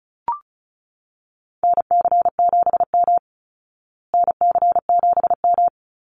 Morse code, heard as a keyed tone, about 34 wpm, 700 Hz. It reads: NC7M NC7M
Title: Call Signs Repeated 3x - 35wpm